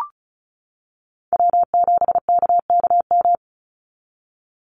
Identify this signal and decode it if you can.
Morse code, heard as a keyed tone, about 34 wpm, 700 Hz. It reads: W7KKM
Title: NA SOTA Chaser Callsigns 35wpm